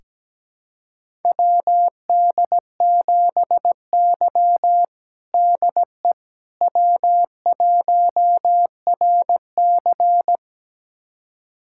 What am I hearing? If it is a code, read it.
Morse code, heard as a keyed tone, about 17 wpm, 700 Hz. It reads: WD7Y DE W1RC
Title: USA Callsign DE Callsign - 4 Characters - 17wpm